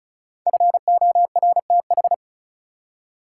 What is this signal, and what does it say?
Morse code, heard as a keyed tone, about 36 wpm, 700 Hz. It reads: FORTH
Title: Five Letter Words - 35wpm